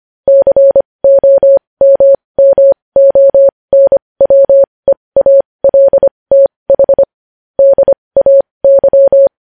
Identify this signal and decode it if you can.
Morse code, heard as a keyed tone, about 25 wpm, spaced as for 20 wpm, 550 Hz. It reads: COMMONWEALTH DAY
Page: News Headlines in Morse Code at 20 WPM